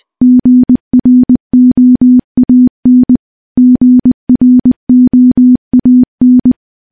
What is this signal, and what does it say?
Morse code, heard as a keyed tone, about 20 wpm, 250 Hz. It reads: GROAN GROAN
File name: morse-training-20WPM